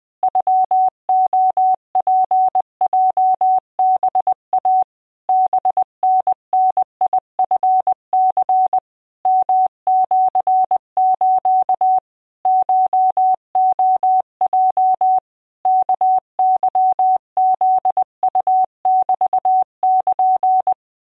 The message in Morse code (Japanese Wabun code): ノレツヲハイ ハタダチニ ヨシス コレヲ ワケフウメル